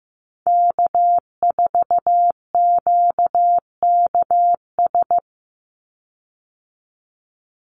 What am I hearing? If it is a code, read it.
Morse code, heard as a keyed tone, about 15 wpm, 700 Hz. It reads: K4QKS
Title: USA Call Signs 15wpm